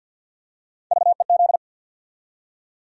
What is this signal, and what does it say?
Morse code, heard as a keyed tone, about 51 wpm, 700 Hz. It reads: VEZ
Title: USA Call Sign Postfixes - 3 Letters 50wpm